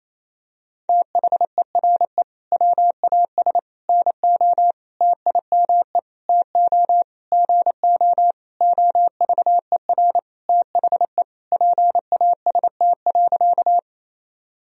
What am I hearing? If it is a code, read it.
Morse code, heard as a keyed tone, about 28 wpm, 700 Hz. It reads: THERE WAS NO TIME TO GO OVER THE PAST.
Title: Sentences from Top 1000 Words - Review Part 3 - 28wpm